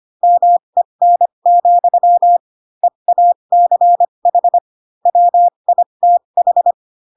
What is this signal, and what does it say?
Morse code, heard as a keyed tone, about 25 wpm, spaced as for 22 wpm, 700 Hz. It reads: MEN, EACH WITH
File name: Pinocchio / Pinocchio ch2 18 25 ik1hge